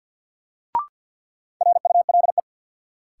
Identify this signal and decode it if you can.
Morse code, heard as a keyed tone, about 53 wpm, 700 Hz. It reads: RULE
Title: Sentences from Top 400 Words - New Word Drill - 50wpm